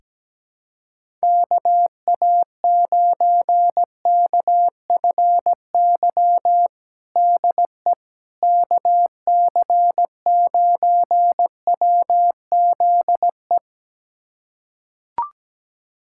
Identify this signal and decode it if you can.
Morse code, heard as a keyed tone, about 17 wpm, 700 Hz. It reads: KA9KFY DE KC9WZE
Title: USA Callsign DE Callsign - 6 Characters - 17wpm